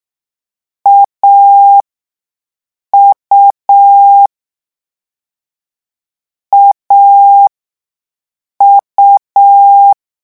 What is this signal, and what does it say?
Morse code, heard as a keyed tone, about 6 wpm, spaced as for 5 wpm, 800 Hz. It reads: AU AU
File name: morse mp3 / lecon3-vitesse 6